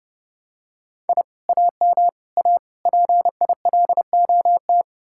Morse code, heard as a keyed tone, about 30 wpm, 700 Hz. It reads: I AM A PILOT